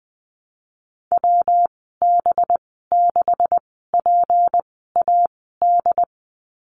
Morse code, heard as a keyed tone, about 19 wpm, spaced as for 14 wpm, 700 Hz. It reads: WB6PAD
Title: 186 Lesson 30g - Letter b - Call Signs 20wpm